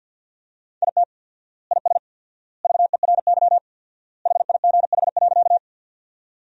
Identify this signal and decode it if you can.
Morse code, heard as a keyed tone, about 50 wpm, 700 Hz. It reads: IT IS VERY HIGH.
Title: 4x Word-Spacing - Sentences from Top 200 Words - 50wpm